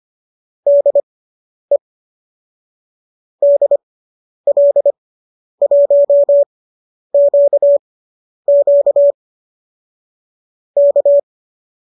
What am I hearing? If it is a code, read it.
Morse code, heard as a keyed tone, about 25 wpm, spaced as for 10 wpm, 570 Hz. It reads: DE DL1QQ K
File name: qso219 10